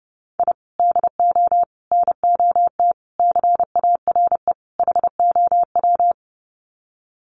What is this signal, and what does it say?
Morse code, heard as a keyed tone, about 28 wpm, 700 Hz. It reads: I DO NOT CARE HOW
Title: Sets of 5 Words Encore2 30wpm